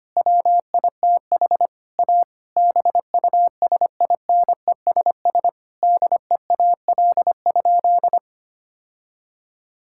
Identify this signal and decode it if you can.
Morse code, heard as a keyed tone, about 25 wpm, 700 Hz. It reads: WITH A BUSINESS DEAL?
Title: Sentences from Top 1000 Words - Review Part 6 - 25wpm